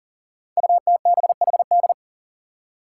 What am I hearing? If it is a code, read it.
Morse code, heard as a keyed tone, about 39 wpm, 700 Hz. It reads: UTBHD
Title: Five Challenging Letter-Number Combinations 40wpm